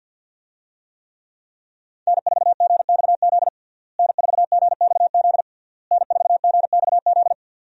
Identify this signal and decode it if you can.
Morse code, heard as a keyed tone, about 48 wpm, 700 Hz. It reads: N4GXZ N4GXZ N4GXZ
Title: USA Call Signs - 5 Characters - Repeated 3x - 50wpm